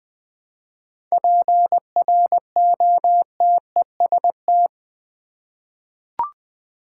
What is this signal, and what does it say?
Morse code, heard as a keyed tone, about 20 wpm, 700 Hz. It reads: PROTEST
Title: 7 Letter Words - First 3 Letters Given - 20wpm